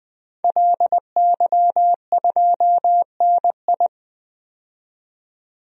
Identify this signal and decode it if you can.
Morse code, heard as a keyed tone, about 21 wpm, 700 Hz. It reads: LY2NI
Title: DXCC-100 Call Signs - 5 Characters - 20wpm